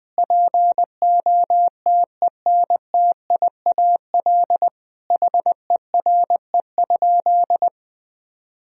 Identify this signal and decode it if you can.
Morse code, heard as a keyed tone, about 20 wpm, 700 Hz. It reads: POTENTIAL HERE?
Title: Sentences from Top 1000 Words - Review Part 4 - 20wpm